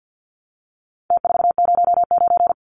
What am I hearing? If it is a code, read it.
Morse code, heard as a keyed tone, about 50 wpm, 700 Hz. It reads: T409
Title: Numbers - Contest Style - 4-Digit 50wpm